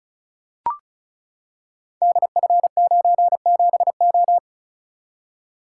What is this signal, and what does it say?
Morse code, heard as a keyed tone, about 36 wpm, 700 Hz. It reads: DF97O
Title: Code Groups - 5 Letter:Numbers - 35wpm